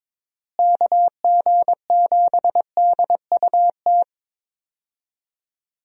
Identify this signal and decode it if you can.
Morse code, heard as a keyed tone, about 22 wpm, 700 Hz. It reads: KG7DUT
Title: USA Call Signs - 6 Characters - Repeated 3x - 22wpm